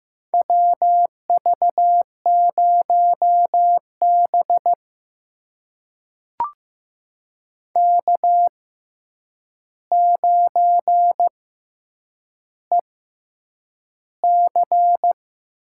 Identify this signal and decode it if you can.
Morse code, heard as a keyed tone, about 15 wpm, spaced as for 8 wpm, 700 Hz. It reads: WV0B K9EC
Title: USA Call Signs - 4 Characters - 6x Character Spacing - 15wpm